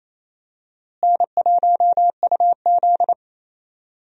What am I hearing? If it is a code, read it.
Morse code, heard as a keyed tone, about 28 wpm, 700 Hz. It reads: N1UZ
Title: USA Call Signs - 4 Characters - Repeated 2x - 28wpm